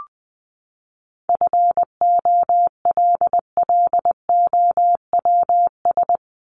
Morse code, heard as a keyed tone, about 20 wpm, 700 Hz. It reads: FOLLOWS
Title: ES:S Suffix - Top 100 Words with Base Words 20wpm